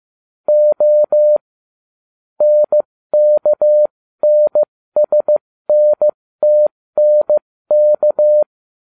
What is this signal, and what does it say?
Morse code, heard as a keyed tone, about 15 wpm, spaced as for 13 wpm, 600 Hz. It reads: O NKNSNTNK